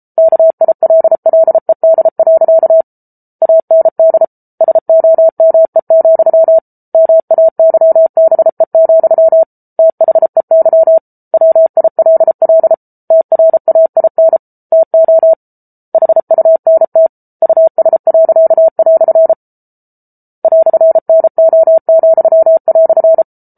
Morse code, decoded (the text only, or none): KILLED. AND SOME, MAYBE, THEY WILL TRAIN TO HUNT US." "NO,"